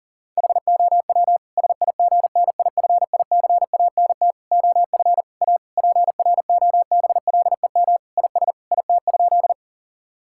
Morse code, heard as a keyed tone, about 37 wpm, 700 Hz. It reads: HOW SIGNIFICANT OF A PROBLEM IS IT?
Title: Sentences from Top 1000 Words - Review Part 6 - 40wpm